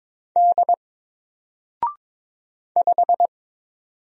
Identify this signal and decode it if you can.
Morse code, heard as a keyed tone, about 22 wpm, 700 Hz. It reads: D 5
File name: Single-Letter-Number-22wpm